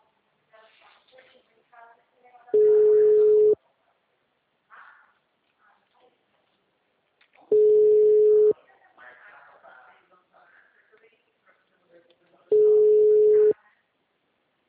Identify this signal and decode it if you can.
Ringback tone (Europe, 425 Hz), with speech.